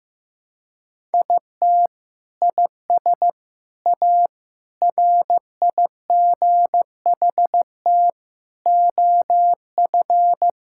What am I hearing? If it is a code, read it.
Morse code, heard as a keyed tone, about 15 wpm, 700 Hz. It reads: IT IS A RIGHT OF
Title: Sentences from Top 1000 Words - Review Part 5 - 15wpm